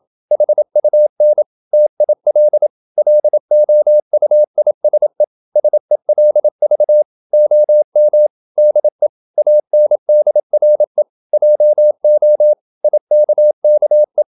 Morse code, heard as a keyed tone, about 27 wpm, 600 Hz. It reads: HUN TIL LOUISE SELV OM DE ANDRE JO IKKE
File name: Ferie QRQ 27wpm